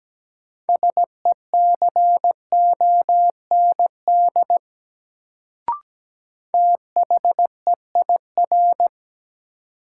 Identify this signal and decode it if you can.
Morse code, heard as a keyed tone, about 17 wpm, 700 Hz. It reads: SECOND THEIR
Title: Sentences from Top 300 Words - New Word Drill - 17wpm